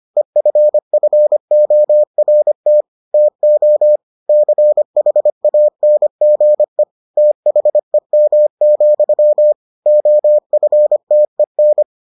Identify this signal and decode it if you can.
Morse code, heard as a keyed tone, about 25 wpm, 600 Hz. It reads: EFFORT TO CHANGE THEM, OFTEN